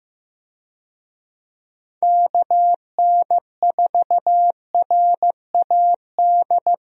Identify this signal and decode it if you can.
Morse code, heard as a keyed tone, about 15 wpm, 700 Hz. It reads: KN4RAD
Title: USA Call Signs - 6 Characters - Repeated 2x - 15wpm